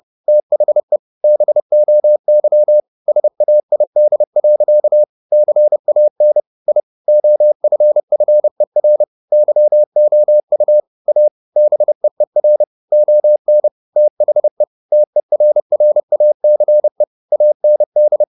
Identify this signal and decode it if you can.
Morse code, heard as a keyed tone, about 30 wpm, 600 Hz. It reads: THE BOY SAID. CAN I OFFER YOU A BEER ON THE TERRACE AND
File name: the old man and the sea / the old man and the sea 1 30wpm